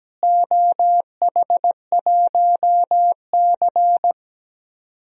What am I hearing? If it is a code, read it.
Morse code, heard as a keyed tone, about 17 wpm, 700 Hz. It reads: OH1C
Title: DXCC-100 Call Signs - 4 Characters - Repeated 3x - 17wpm